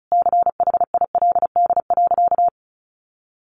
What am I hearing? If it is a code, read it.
Morse code, heard as a keyed tone, about 34 wpm, 700 Hz. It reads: CHILD.
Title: Sentences from Top 1300 Words 35wpm